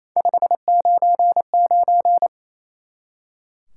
Morse code, heard as a keyed tone, about 36 wpm, 700 Hz. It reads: 599